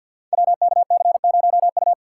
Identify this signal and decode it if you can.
Morse code, heard as a keyed tone, about 51 wpm, 700 Hz. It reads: WKC0U
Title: Code Groups - 5 Letter:Numbers - 50wpm